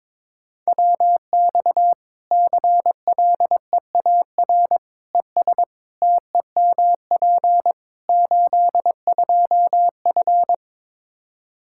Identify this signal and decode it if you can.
Morse code, heard as a keyed tone, about 22 wpm, 700 Hz. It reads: WX CLEAR ES TEMP 82F
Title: Practice QSO - Exchange 2 - Info - 22wpm